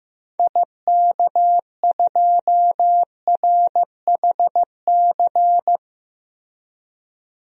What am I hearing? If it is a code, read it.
Morse code, heard as a keyed tone, about 15 wpm, 700 Hz. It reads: IK2RHC
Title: DXCC-100 Call Signs - 6 Characters - Repeated 2x - 15wpm